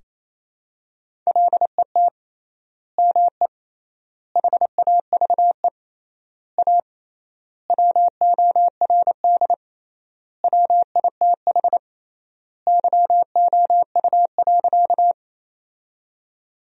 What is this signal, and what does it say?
Morse code, heard as a keyed tone, about 27 wpm, 700 Hz. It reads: LET ME HAVE A WORD WITH YOU.
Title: 3x Word-Spacing - Sentences from Top 300 Words - 28wpm